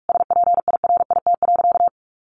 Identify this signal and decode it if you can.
Morse code, heard as a keyed tone, about 44 wpm, 700 Hz. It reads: SPIRIT.
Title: Sentences from Top 1400 Words 45wpm